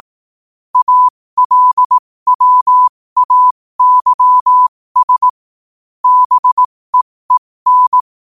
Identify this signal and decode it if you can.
Morse code, heard as a keyed tone, about 18 wpm, spaced as for 15 wpm, 1000 Hz. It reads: ALWAYS BEEN
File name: 15wpm